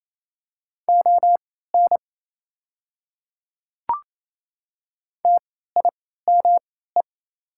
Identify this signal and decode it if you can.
Morse code, heard as a keyed tone, about 27 wpm, spaced as for 16 wpm, 690 Hz. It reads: ON TIME
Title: 3x Character Spacing - Top 100 Words - 28wpm